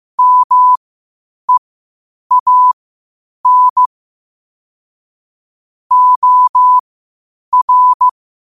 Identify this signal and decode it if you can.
Morse code, heard as a keyed tone, about 15 wpm, spaced as for 8 wpm, 1000 Hz. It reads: MEAN OR
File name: qotdfiles / 8wpm